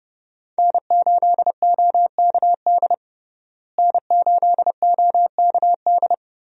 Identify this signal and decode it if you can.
Morse code, heard as a keyed tone, about 30 wpm, 700 Hz. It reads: N8OKD N8OKD
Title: USA Call Signs - 5 Characters - Repeated 2x - 30wpm